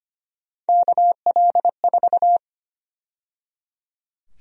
Morse code, heard as a keyed tone, about 25 wpm, 700 Hz. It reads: KL4
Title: USA Call Sign Prefixes - 25wpm